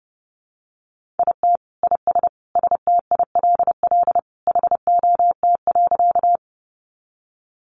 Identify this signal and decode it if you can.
Morse code, heard as a keyed tone, about 30 wpm, 700 Hz. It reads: IT IS STILL HOT.
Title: Easy Phrases 30wpm